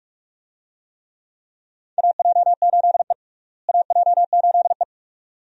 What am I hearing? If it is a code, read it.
Morse code, heard as a keyed tone, about 41 wpm, 700 Hz. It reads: AJ8E AJ8E
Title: USA Call Signs - 4 Characters - Repeated 2x - 45wpm